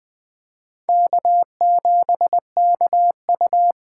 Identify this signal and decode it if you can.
Morse code, heard as a keyed tone, about 21 wpm, 700 Hz. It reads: K7KU
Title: Speed Racing - USA Call Signs 17wpm